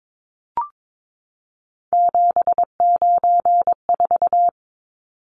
Morse code, heard as a keyed tone, about 22 wpm, 700 Hz. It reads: 794
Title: Numbers - 3-Digit 22wpm